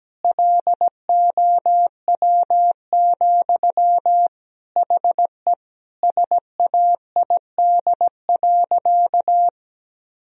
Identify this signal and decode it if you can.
Morse code, heard as a keyed tone, about 17 wpm, 700 Hz. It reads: LOW, HE SAID.
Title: Sentences from Top 1000 Words - Review Part 1 - 17wpm